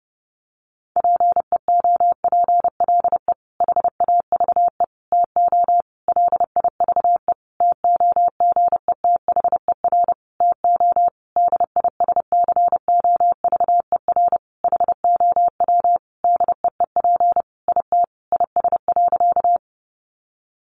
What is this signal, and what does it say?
Morse code, heard as a keyed tone, about 30 wpm, 700 Hz. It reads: PEOPLE HAVE TO LIVE TOGETHER TO DISCOVER HOW DEEP IT IS.
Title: Sentences from Top 1100 Words 30wpm